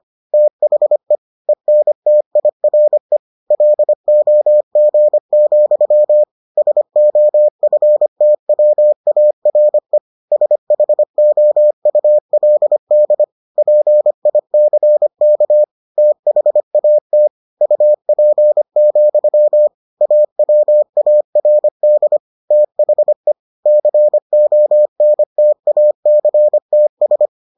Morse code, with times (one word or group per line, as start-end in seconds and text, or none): 0.3-1.2 THE
1.5-3.2 ENTIRE
3.5-6.2 LOG,
6.6-10.0 SOFTWARE
10.3-13.2 SHOULD
13.6-15.6 PICK
16.0-17.3 THAT
17.6-19.7 UP,
20.0-22.2 AWARD
22.5-23.3 THE
23.7-27.3 CONTACTS